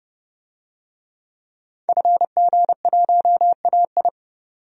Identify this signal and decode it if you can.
Morse code, heard as a keyed tone, about 32 wpm, 700 Hz. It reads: FG1AI
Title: DXCC-100 Call Signs - 5 Characters - 30wpm